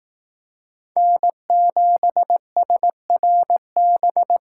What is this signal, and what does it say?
Morse code, heard as a keyed tone, about 19 wpm, 700 Hz. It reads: N7SRB
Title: Speed Racing - USA 1x3 Call Signs 15wpm